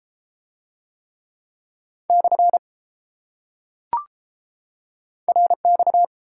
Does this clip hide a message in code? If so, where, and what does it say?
Yes: Morse code from 2.1 to 6.0 s, heard as a keyed tone, about 31 wpm, 700 Hz. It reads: / RX